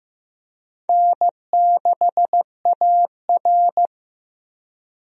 Morse code, heard as a keyed tone, about 15 wpm, 700 Hz. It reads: N6AR
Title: USA Call Signs - 4 Characters - Repeated 2x - 15wpm